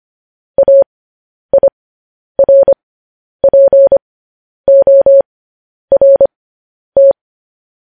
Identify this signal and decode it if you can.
Morse code, heard as a keyed tone, about 25 wpm, spaced as for 10 wpm, 550 Hz. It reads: AIRPORT
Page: news headlines in morse code at 10 wpm